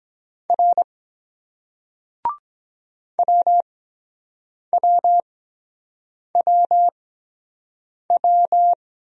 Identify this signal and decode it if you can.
Morse code, heard as a keyed tone, about 23 wpm, spaced as for 7 wpm, 700 Hz. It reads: R WWWW